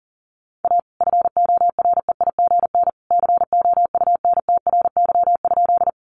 Morse code, heard as a keyed tone, about 37 wpm, 700 Hz. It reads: A FOREIGN COUNTRY?
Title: Sentences from Top 1400 Words 40wpm